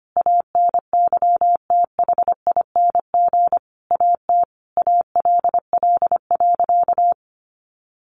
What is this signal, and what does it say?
Morse code, heard as a keyed tone, about 25 wpm, 700 Hz. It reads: ANYTHING AT ALL.
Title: Sentences from Top 1300 Words 25wpm